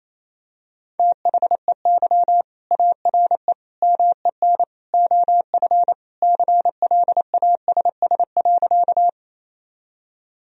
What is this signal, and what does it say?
Morse code, heard as a keyed tone, about 28 wpm, 700 Hz. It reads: THEY ARE MEN OF CLASS.
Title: Sentences from Top 1000 Words - Review Part 2 - 28wpm